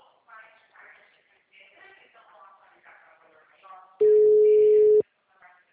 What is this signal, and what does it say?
Ringback tone (Europe, 425 Hz), with speech.